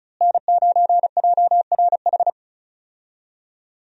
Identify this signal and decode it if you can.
Morse code, heard as a keyed tone, about 36 wpm, 700 Hz. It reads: N9JRH
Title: USA Call Signs - 5 Characters - Repeated 2x - 35wpm